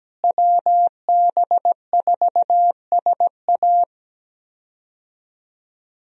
Morse code, heard as a keyed tone, about 17 wpm, 700 Hz. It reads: WB4SA
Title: USA Call Signs - 5 Characters - Repeated 2x - 17wpm